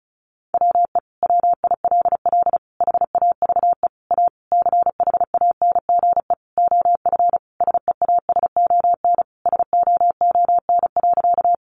Morse code, heard as a keyed tone, about 34 wpm, 700 Hz. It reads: WE WILL HAVE A CHANGE OF SEASON SOON.
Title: Sentences from Top 1400 Words 35wpm